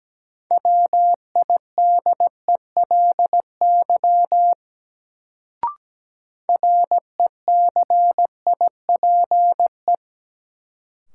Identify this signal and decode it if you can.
Morse code, heard as a keyed tone, about 17 wpm, 700 Hz. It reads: WIDELY RECIPE
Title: Sentences from Top 1200 Words - New Word Drill - 17wpm